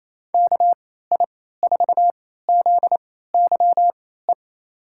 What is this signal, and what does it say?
Morse code, heard as a keyed tone, about 28 wpm, spaced as for 16 wpm, 700 Hz. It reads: KI4ZYE
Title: USA Call Signs - 6 Characters - 3x Character Spacing - 28wpm